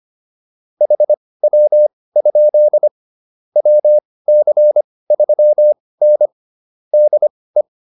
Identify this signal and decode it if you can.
Morse code, heard as a keyed tone, about 25 wpm, spaced as for 18 wpm, 600 Hz. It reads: HW? WC3N DE
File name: QSO209 18